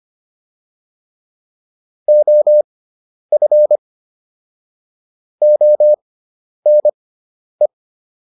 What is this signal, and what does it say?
Morse code, heard as a keyed tone, about 25 wpm, spaced as for 10 wpm, 600 Hz. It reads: OF ONE